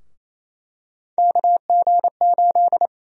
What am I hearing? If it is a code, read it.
Morse code, heard as a keyed tone, about 28 wpm, 700 Hz. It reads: KG8
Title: USA Call Sign Prefixes - 28wpm